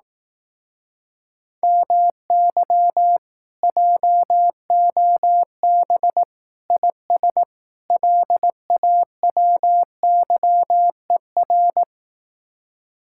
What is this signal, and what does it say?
Morse code, heard as a keyed tone, about 18 wpm, 700 Hz. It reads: MY JOB IS LAWYER
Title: Speed-Racing Common QSO Phrases Encore - 15wpm